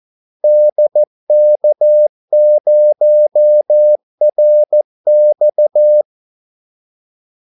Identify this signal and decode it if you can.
Morse code, heard as a keyed tone, about 14 wpm, 600 Hz. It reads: DK0RX
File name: Callsigns-14wpm